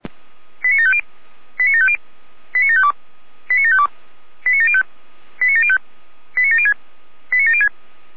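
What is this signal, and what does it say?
Five-tone selective call (ZVEI), heard as quick runs of five tones, tones of about 70 ms. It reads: 89750 89750 89752 89752 89705 89705 89706 89706
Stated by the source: ZVEI five-tone call received by SDR.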